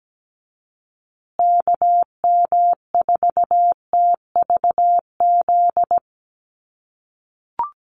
Morse code, heard as a keyed tone, about 17 wpm, 700 Hz. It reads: KM4TVZ
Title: USA Call Signs Repeated 3x 17wpm